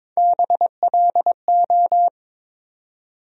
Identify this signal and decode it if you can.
Morse code, heard as a keyed tone, about 22 wpm, 700 Hz. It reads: BLO
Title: USA Call Sign Postfixes - 3 Letters 22wpm